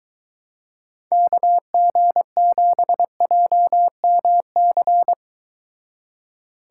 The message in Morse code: KG7JMC

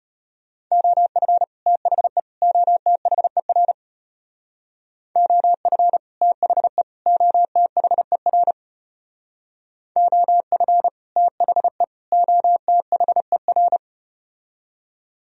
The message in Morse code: OF THE OTHER OF THE OTHER OF THE OTHER

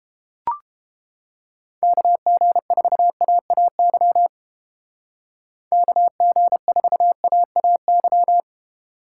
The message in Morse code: KG4AAY KG4AAY

